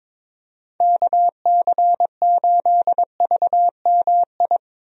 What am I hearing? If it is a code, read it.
Morse code, heard as a keyed tone, about 22 wpm, 700 Hz. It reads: KC8VMI
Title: USA Call Signs - 6 Characters - Repeated 3x - 22wpm